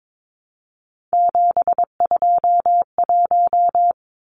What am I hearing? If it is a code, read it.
Morse code, heard as a keyed tone, about 22 wpm, 700 Hz. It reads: 721
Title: Numbers - 3-Digit 22wpm